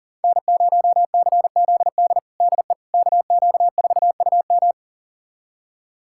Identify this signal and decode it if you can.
Morse code, heard as a keyed tone, about 40 wpm, 700 Hz. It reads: N0CZD DE KQ4UM
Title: USA Callsign DE Callsign - 5 Characters - 40wpm